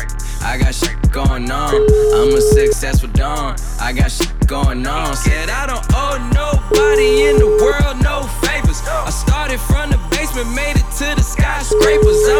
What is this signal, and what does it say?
Ringback tone (Europe, 425 Hz), with noise.